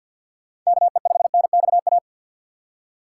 Morse code, heard as a keyed tone, about 48 wpm, 700 Hz. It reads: KE5NXA
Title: USA Call Signs - 2x Character Spacing - 50wpm